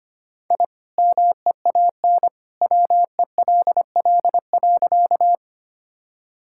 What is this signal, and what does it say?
Morse code, heard as a keyed tone, about 25 wpm, 700 Hz. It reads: I MEAN WELL.
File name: practice / 25wpm / Sentences with 2-Word Phrases - 25wpm